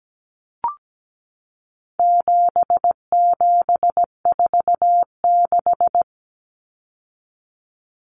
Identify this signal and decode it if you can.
Morse code, heard as a keyed tone, about 17 wpm, 700 Hz. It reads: 7746
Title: Numbers - 4-Digit 17wpm